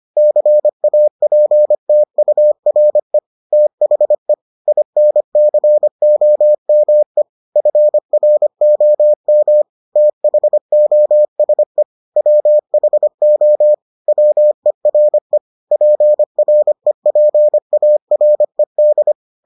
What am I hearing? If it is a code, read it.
Morse code, heard as a keyed tone, about 25 wpm, 600 Hz. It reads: CAPTURE THE INCOME FROM THOSE WHO WERE PREPARED